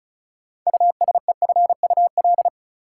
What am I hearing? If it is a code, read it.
Morse code, heard as a keyed tone, about 34 wpm, 700 Hz. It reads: USEFUL
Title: Sentences from Top 700 Words - New Word Drill - 35wpm